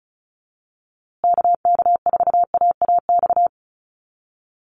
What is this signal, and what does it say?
Morse code, heard as a keyed tone, about 34 wpm, 700 Hz. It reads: KK4AAX
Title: USA Call Signs Repeated 3x 35wpm